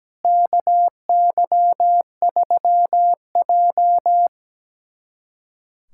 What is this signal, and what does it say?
Morse code, heard as a keyed tone, about 17 wpm, 700 Hz. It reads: KY3J
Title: USA Callsign DE Callsign - 4 Characters - 17wpm